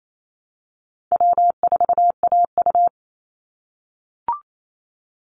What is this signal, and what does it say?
Morse code, heard as a keyed tone, about 28 wpm, 700 Hz. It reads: W4AU
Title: Call Signs - 28wpm